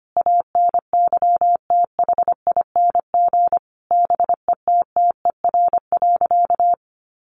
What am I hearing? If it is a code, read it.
Morse code, heard as a keyed tone, about 25 wpm, 700 Hz. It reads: ANYTHING BETTER.
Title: Sentences from Top 1100 Words 25wpm